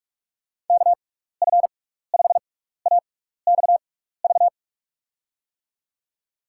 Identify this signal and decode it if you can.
Morse code, heard as a keyed tone, about 42 wpm, spaced as for 15 wpm, 700 Hz. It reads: KF5AXV
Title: USA Call Signs - 6 Characters - 6x Character Spacing - 45wpm